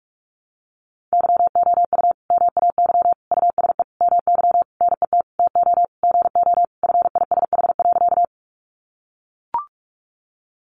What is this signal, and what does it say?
Morse code, heard as a keyed tone, about 44 wpm, 700 Hz. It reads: YOU MAY USE MY NET TO GO FISH.
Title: Sentences from Top 1400 Words 45wpm